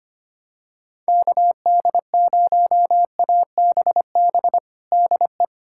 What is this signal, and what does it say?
Morse code, heard as a keyed tone, about 25 wpm, 700 Hz. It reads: KD0ABB DE
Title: USA Callsign DE Callsign - 6 Characters - 25wpm